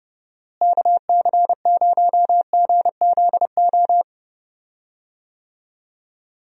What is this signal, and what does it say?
Morse code, heard as a keyed tone, about 31 wpm, 700 Hz. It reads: KC0GZO
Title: USA Call Signs - 6 Characters - Repeated 2x - 30wpm